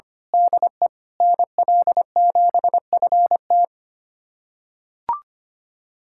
Morse code, heard as a keyed tone, about 25 wpm, 700 Hz. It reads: DE NL7FT
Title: USA Callsign DE Callsign - 5 Characters - 25wpm